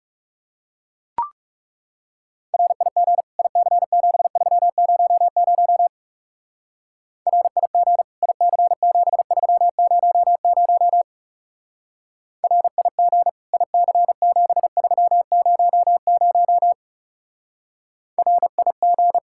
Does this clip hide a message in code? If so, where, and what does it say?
Yes: Morse code from 2.5 to 19.2 s, heard as a keyed tone, about 39 wpm, 700 Hz. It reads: RIG IC7300 RIG IC7300 RIG IC7300 RIG